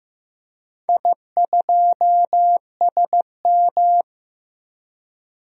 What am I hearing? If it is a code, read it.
Morse code, heard as a keyed tone, about 15 wpm, 700 Hz. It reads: I2SM
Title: DXCC-100 Call Signs - 4 Characters - Repeated 3x - 15wpm